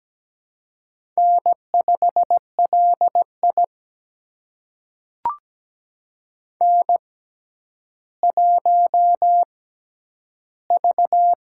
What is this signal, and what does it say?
Morse code, heard as a keyed tone, about 17 wpm, spaced as for 10 wpm, 700 Hz. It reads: N5LI N1V
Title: USA Call Signs - 4 Characters - 6x Character Spacing - 17wpm